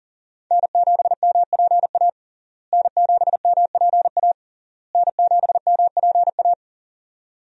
Morse code, heard as a keyed tone, about 40 wpm, 700 Hz. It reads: N7MPA N7MPA N7MPA
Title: USA Call Signs - 5 Characters - Repeated 3x - 40wpm